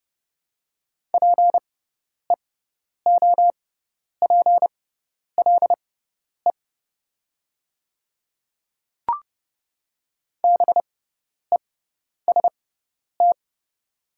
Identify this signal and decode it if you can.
Morse code, heard as a keyed tone, about 30 wpm, spaced as for 10 wpm, 700 Hz. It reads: PEOPLE BEST
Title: 6x Character Spacing - Top 500 Words - 30wpm